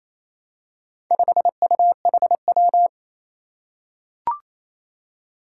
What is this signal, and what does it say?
Morse code, heard as a keyed tone, about 27 wpm, 700 Hz. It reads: 5UHW
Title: Four Challenging Letter-Number Combinations 28wpm